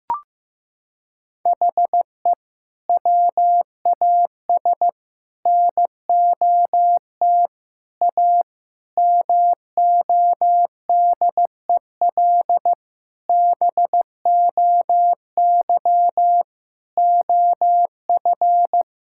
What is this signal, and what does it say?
Morse code, heard as a keyed tone, about 15 wpm, 700 Hz. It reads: HE WAS NOT A MODEL BOY OF
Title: Sentences from Top 1000 Words - Review Part 5 - 15wpm